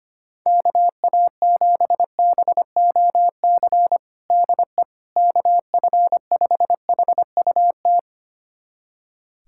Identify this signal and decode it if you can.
Morse code, heard as a keyed tone, about 25 wpm, 700 Hz. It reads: KA7BOC DE KF5HUT
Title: USA Callsign DE Callsign - 6 Characters - 25wpm